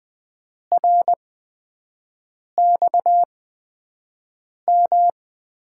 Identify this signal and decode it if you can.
Morse code, heard as a keyed tone, about 20 wpm, spaced as for 5 wpm, 700 Hz. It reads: RXM
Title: USA Call Signs - 6 Characters - 8x Character Spacing - 20wpm